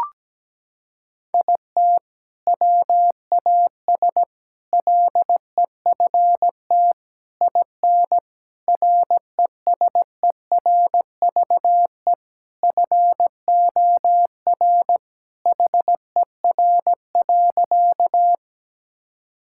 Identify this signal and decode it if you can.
Morse code, heard as a keyed tone, about 17 wpm, 700 Hz. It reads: IT WAS LEFT IN RESERVE FOR HER.
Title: Sentences from Top 1000 Words - Review Part 6 - 17wpm